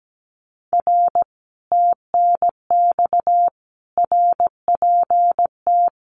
Morse code, heard as a keyed tone, about 17 wpm, 700 Hz. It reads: R TNX RPT
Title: Common QSO Phrases Encore 17wpm